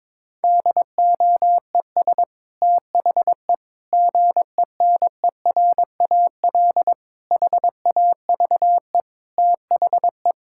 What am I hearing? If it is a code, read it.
Morse code, heard as a keyed tone, about 22 wpm, 700 Hz. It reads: DOES THE GENERAL HAVE THE
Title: Sentences from Top 1000 Words - Review Part 6 - 22wpm